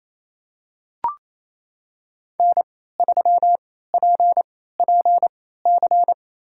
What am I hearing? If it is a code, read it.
Morse code, heard as a keyed tone, about 28 wpm, spaced as for 16 wpm, 700 Hz. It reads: N3PPC
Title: USA Call Signs - 3x Character Spacing - 28wpm